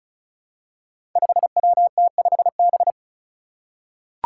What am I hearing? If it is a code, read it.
Morse code, heard as a keyed tone, about 36 wpm, 700 Hz. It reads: 5WT5B
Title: Five Challenging Letter-Number Combinations 35wpm